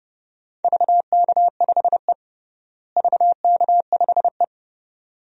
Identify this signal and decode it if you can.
Morse code, heard as a keyed tone, about 31 wpm, 700 Hz. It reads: VK5E VK5E
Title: DXCC-100 Call Signs - 4 Characters - Repeated 2x - 30wpm